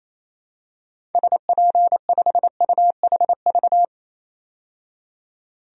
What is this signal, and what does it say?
Morse code, heard as a keyed tone, about 28 wpm, 700 Hz. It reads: SP5UHV
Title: DXCC-100 Call Signs - 6 Characters - 28wpm